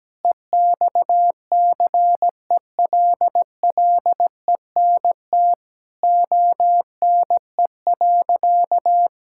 Morse code, heard as a keyed tone, about 17 wpm, 700 Hz. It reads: EXCELLENT ONE.
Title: Sentences from Top 1000 Words - Review Part 6 - 17wpm